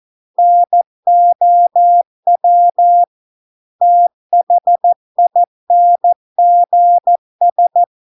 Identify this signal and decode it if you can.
Morse code, heard as a keyed tone, about 14 wpm, 700 Hz. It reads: NOW THINGS